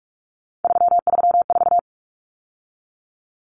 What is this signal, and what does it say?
Morse code, heard as a keyed tone, about 44 wpm, 700 Hz. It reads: 334
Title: Numbers - 3-Digit 45wpm